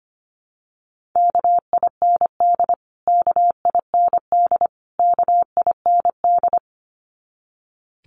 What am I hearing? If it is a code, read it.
Morse code, heard as a keyed tone, about 25 wpm, 700 Hz. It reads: KIND KIND KIND
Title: Top 500 Words Repeated 25wpm